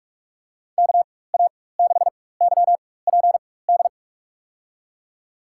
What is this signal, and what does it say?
Morse code, heard as a keyed tone, about 46 wpm, spaced as for 21 wpm, 700 Hz. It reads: KA6YPD
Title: USA Call Signs - 6 Characters - 4x Character Spacing - 45wpm